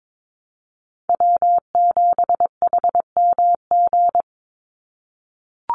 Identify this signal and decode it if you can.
Morse code, heard as a keyed tone, about 22 wpm, 700 Hz. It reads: W7HMG
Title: USA Call Signs 22wpm